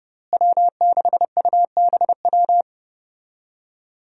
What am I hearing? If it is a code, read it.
Morse code, heard as a keyed tone, about 30 wpm, 700 Hz. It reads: W6UBW
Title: USA Call Signs - 5 Characters - Repeated 3x - 30wpm